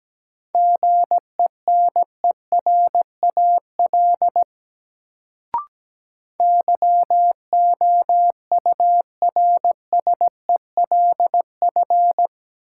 Morse code, heard as a keyed tone, about 17 wpm, 700 Hz. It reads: GENERAL YOURSELF
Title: Sentences from Top 600 Words - New Word Drill - 17wpm